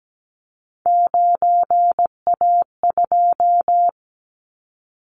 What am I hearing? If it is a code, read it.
Morse code, heard as a keyed tone, about 17 wpm, 700 Hz. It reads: 9A2
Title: Callsign Prefixes for Top 50 DXCC - 17wpm